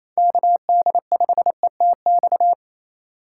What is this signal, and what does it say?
Morse code, heard as a keyed tone, about 28 wpm, 700 Hz. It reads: KD5ETX
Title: USA Call Signs - 6 Characters - 6x Character Spacing - 28wpm